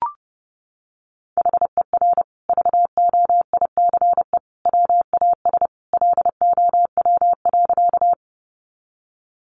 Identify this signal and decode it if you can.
Morse code, heard as a keyed tone, about 29 wpm, 700 Hz. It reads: HER VOICE WAS LOW.
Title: Sentences from Top 500 Words 30wpm